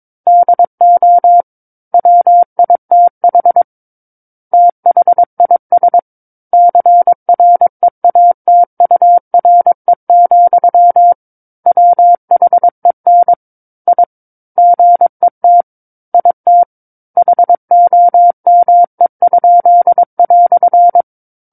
Morse code, heard as a keyed tone, about 22 wpm, 700 Hz. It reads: DO WITH THIS CREATURE, WHEN I GET IT HOME?"
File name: 06-Chapter-06 22 WPM